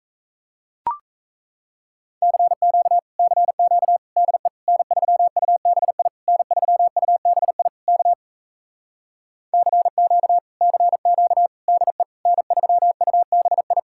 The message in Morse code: CQ CQ DE N3UBI N3UBI K CQ CQ DE N3UBI

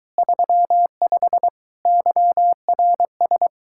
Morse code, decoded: 35 YRS

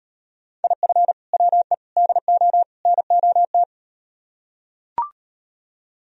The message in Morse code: IF WE DO NOT

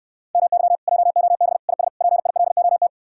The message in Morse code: MY JOB IS PILOT

